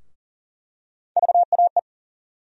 Morse code, heard as a keyed tone, about 40 wpm, 700 Hz. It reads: VAE
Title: USA Call Sign Postfixes - 3 Letters 40wpm